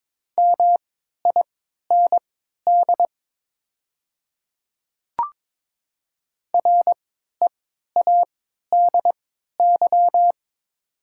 Morse code, heard as a keyed tone, about 22 wpm, spaced as for 12 wpm, 700 Hz. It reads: MIND READY
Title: 3x Character Spacing - Top 500 Words - 22wpm